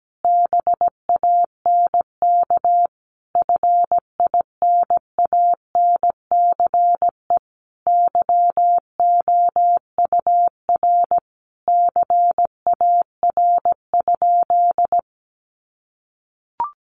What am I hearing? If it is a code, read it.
Morse code, heard as a keyed tone, about 17 wpm, 700 Hz. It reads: BANK FINANCE YOUR CAR?
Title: Sentences from Top 1500 Words 17wpm